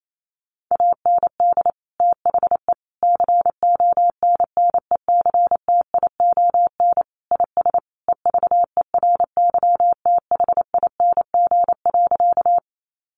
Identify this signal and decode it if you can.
Morse code, heard as a keyed tone, about 28 wpm, 700 Hz. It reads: AND THE CONNECTION IS EVERYTHING.
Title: Sentences from Top 1300 Words 28wpm